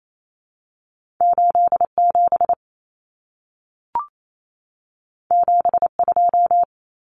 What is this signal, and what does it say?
Morse code, heard as a keyed tone, about 28 wpm, 700 Hz. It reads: 87 72
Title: Numbers - 2-Digit 28wpm